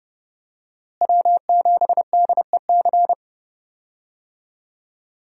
Morse code, heard as a keyed tone, about 30 wpm, 700 Hz. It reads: W7DEC
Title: USA Call Signs - 5 Characters - Repeated 2x - 30wpm